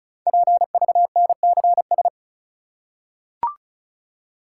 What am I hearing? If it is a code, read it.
Morse code, heard as a keyed tone, about 36 wpm, 700 Hz. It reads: PVNCS
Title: Code Groups - 5 Letter:Numbers - 35wpm